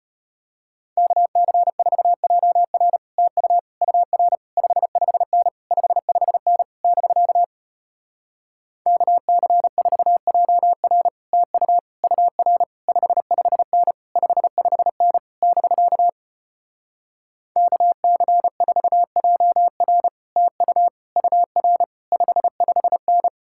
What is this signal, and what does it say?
Morse code, heard as a keyed tone, about 33 wpm, 700 Hz. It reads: KC4JR TU UR 55N 55N <BK> KC4JR TU UR 55N 55N <BK> KC4JR TU UR 55N